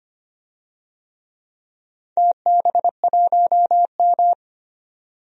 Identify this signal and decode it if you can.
Morse code, heard as a keyed tone, about 25 wpm, 700 Hz. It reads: TB1M
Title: DXCC-100 Call Signs - 4 Characters - 25wpm